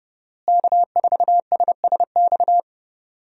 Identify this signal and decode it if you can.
Morse code, heard as a keyed tone, about 30 wpm, 700 Hz. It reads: K4SSX
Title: USA Call Signs - 5 Characters - Repeated 3x - 30wpm